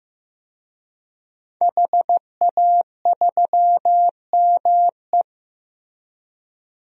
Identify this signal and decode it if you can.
Morse code, heard as a keyed tone, about 15 wpm, 700 Hz. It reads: HA3ME
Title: DXCC-100 Call Signs - 5 Characters - 15wpm